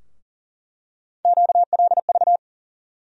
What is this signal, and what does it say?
Morse code, heard as a keyed tone, about 40 wpm, 700 Hz. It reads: QLV
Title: USA Call Sign Postfixes - 3 Letters 40wpm